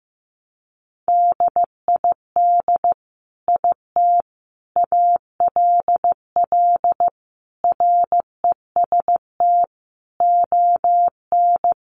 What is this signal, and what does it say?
Morse code, heard as a keyed tone, about 15 wpm, 700 Hz. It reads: DID IT ALL REST ON
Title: Sentences from Top 500 Words 15wpm